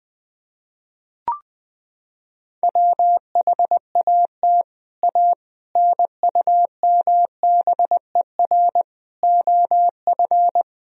Morse code, heard as a keyed tone, about 20 wpm, 700 Hz. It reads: WHAT A NUMBER OF
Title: Sentences from Top 1000 Words - Review Part 1 - 20wpm